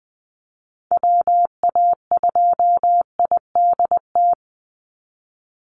Morse code, heard as a keyed tone, about 20 wpm, 700 Hz. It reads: WA2IDT
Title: USA Call Signs 20wpm